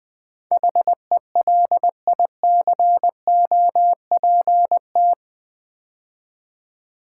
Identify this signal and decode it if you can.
Morse code, heard as a keyed tone, about 20 wpm, 700 Hz. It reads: HELICOPT
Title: Build A Word - 4 Letters At A Time - 10-Letter Words - 20wpm